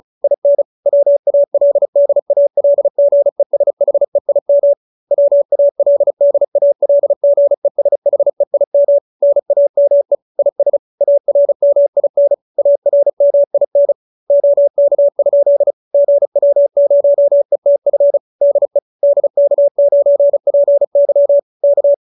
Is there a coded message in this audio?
Yes: Morse code from 0.2 to 21.9 s, heard as a keyed tone, about 34 wpm, 570 Hz. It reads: IN WALDALGESHEIM WALDALGESHEIM NAME IS ARMIN ARMIN OK? GW0ETF DE DK9PY K